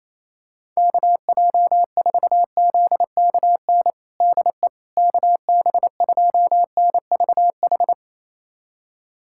Morse code, heard as a keyed tone, about 28 wpm, 700 Hz. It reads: KJ4ZKN DE KB2NVH
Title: USA Callsign DE Callsign - 6 Characters - 28wpm